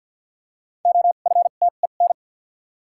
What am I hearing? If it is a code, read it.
Morse code, heard as a keyed tone, about 52 wpm, spaced as for 37 wpm, 700 Hz. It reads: OFTEN
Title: 2x Character Spacing - Top 500 Words - 50wpm